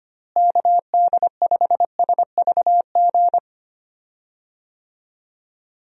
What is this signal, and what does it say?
Morse code, heard as a keyed tone, about 25 wpm, 700 Hz. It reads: KD5SVG
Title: USA Call Signs - 6 Characters - Repeated 2x - 25wpm